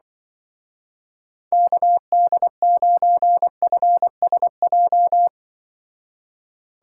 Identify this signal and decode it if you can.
Morse code, heard as a keyed tone, about 23 wpm, 700 Hz. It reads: KD9FSJ
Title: Speed Racing - USA 2x3 Call Signs 20wpm